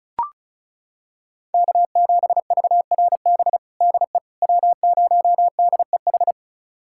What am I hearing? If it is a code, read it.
Morse code, heard as a keyed tone, about 36 wpm, 700 Hz. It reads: K7VRB DE W0DEH
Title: USA Callsign DE Callsign - 5 Characters - 35wpm